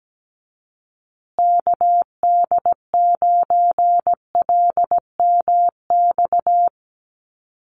Morse code, heard as a keyed tone, about 17 wpm, 700 Hz. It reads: KD9LMX
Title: USA Call Signs Repeated 3x 17wpm